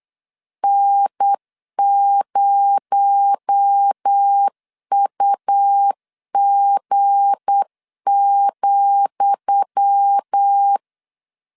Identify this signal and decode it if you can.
Morse code, heard as a keyed tone, about 8 wpm, 800 Hz. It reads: N0UG,
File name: LEZIONI / 10 SIGNS / 10D COMA 8wpm 3749